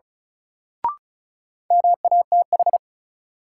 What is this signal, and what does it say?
Morse code, heard as a keyed tone, about 36 wpm, 700 Hz. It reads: MATH
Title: Sentences from Top 1000 Words - Review Part 6 - New Word Drill - 35wpm